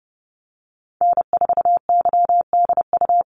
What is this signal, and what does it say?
Morse code, heard as a keyed tone, about 30 wpm, 700 Hz. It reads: N4YDU
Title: Call Signs - 30wpm